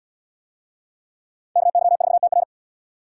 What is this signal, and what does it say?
Morse code, heard as a keyed tone, about 73 wpm, 700 Hz. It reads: KY4TR